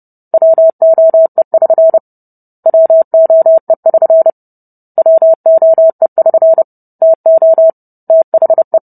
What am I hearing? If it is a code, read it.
Morse code, heard as a keyed tone, about 30 wpm, 650 Hz. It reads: WOE<SN> WOE<SN> WOE<SN> TO THE